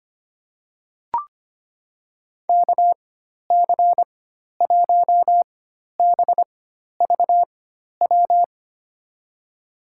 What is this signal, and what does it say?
Morse code, heard as a keyed tone, about 25 wpm, spaced as for 12 wpm, 700 Hz. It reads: KC1BVW
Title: USA Call Signs - 6 Characters - 4x Character Spacing - 25wpm